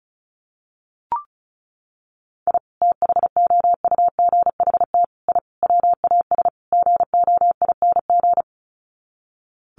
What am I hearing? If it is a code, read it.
Morse code, heard as a keyed tone, about 34 wpm, 700 Hz. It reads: I THOUGHT I WAS GOING
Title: Sets of 5 Words Encore2 35wpm